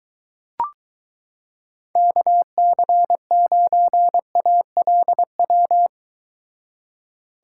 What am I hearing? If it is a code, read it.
Morse code, heard as a keyed tone, about 22 wpm, 700 Hz. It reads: KC9ALW